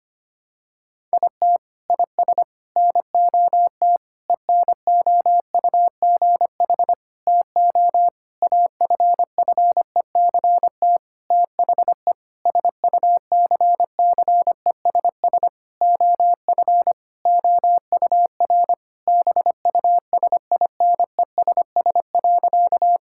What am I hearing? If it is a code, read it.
Morse code, heard as a keyed tone, about 25 wpm, 700 Hz. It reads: IT IS NOT ENOUGH TO AFFECT THE SUCCESS OF OUR BUSINESS.
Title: Sentences from Top 1000 Words - Review Part 6 - 25wpm